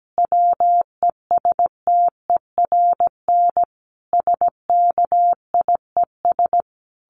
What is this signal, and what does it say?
Morse code, heard as a keyed tone, about 17 wpm, 700 Hz. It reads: WESTERN SKIES
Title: A Sound Of Thunder 17wpm